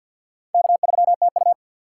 Morse code, heard as a keyed tone, about 52 wpm, 700 Hz. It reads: K3TU
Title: USA Call Signs - 4 Characters - 3x Character Spacing - 50wpm